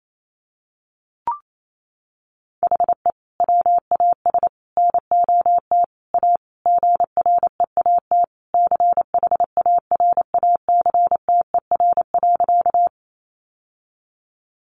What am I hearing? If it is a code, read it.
Morse code, heard as a keyed tone, about 28 wpm, 700 Hz. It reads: HE WAS NOT A GREAT CHARACTER.
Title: Sentences from Top 1200 Words 28wpm